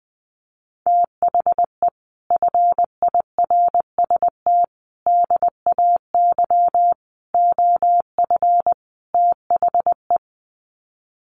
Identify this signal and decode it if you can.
Morse code, heard as a keyed tone, about 20 wpm, 700 Hz. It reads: THE FIRST DAY OF THE
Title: Sets of 5 Words Encore2 20wpm